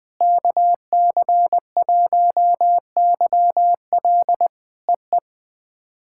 Morse code, heard as a keyed tone, about 20 wpm, 700 Hz. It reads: KC1YL EE
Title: POTA QSO Practice - 20wpm